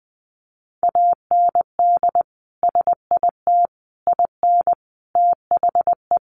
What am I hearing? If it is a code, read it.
Morse code, heard as a keyed tone, about 20 wpm, 700 Hz. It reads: AND SIT IN THE
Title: Sentences from Top 500 Words 20wpm